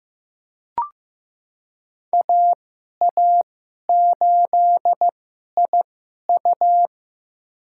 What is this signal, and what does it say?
Morse code, heard as a keyed tone, about 15 wpm, spaced as for 11 wpm, 700 Hz. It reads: AA8IU
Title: USA Call Signs - 2x Character Spacing - 15wpm